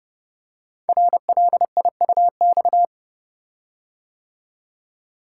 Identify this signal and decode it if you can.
Morse code, heard as a keyed tone, about 29 wpm, 700 Hz. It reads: RLIUX
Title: Code Groups - 5 Letter:Numbers - 30wpm